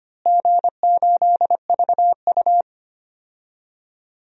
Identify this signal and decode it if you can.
Morse code, heard as a keyed tone, about 25 wpm, 700 Hz. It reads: G8VU
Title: DXCC-100 Call Signs - 4 Characters - Repeated 2x - 25wpm